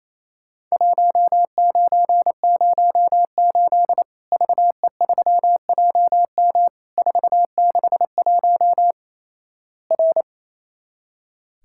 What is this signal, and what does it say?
Morse code, heard as a keyed tone, about 28 wpm, 700 Hz. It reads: 1908 VE3JM 461 R
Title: Worked All Europe DX Contest - QTC Exchange - 28wpm